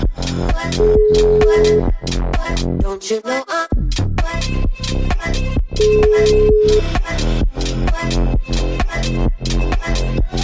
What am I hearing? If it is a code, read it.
Ringback tone (Europe, 425 Hz), with noise.